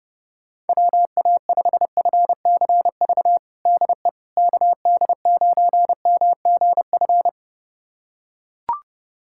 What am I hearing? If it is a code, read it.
Morse code, heard as a keyed tone, about 32 wpm, 700 Hz. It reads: WA5FCV DE KD9MGF
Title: USA Callsign DE Callsign - 6 Characters - 30wpm